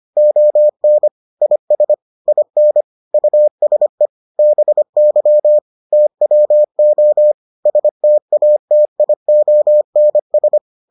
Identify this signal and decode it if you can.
Morse code, heard as a keyed tone, about 25 wpm, 600 Hz. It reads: ON IS IN USE BY TWO STATIONS